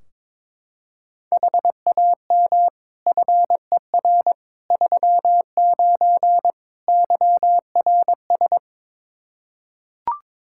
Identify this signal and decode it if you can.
Morse code, heard as a keyed tone, about 22 wpm, 700 Hz. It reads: HAM FER 39 YRS
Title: Practice QSO - Exchange 3 - More Info - 22wpm